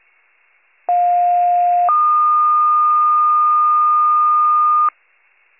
Two-tone page, heard as a one-second tone, then a three-second tone, about 710 then 1220 Hz. It is heard from 0.9 to 4.9 s.